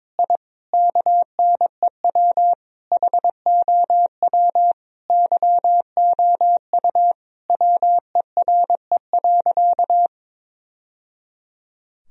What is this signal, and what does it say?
Morse code, heard as a keyed tone, about 22 wpm, 700 Hz. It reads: I KNEW HOW YOU WERE.
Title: Sentences from Top 1000 Words - Review Part 1 - 22wpm